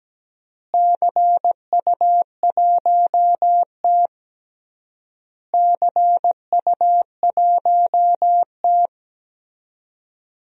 Morse code, heard as a keyed tone, about 17 wpm, 700 Hz. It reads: CU1T CU1T
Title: DXCC-100 Call Signs - 4 Characters - Repeated 3x - 17wpm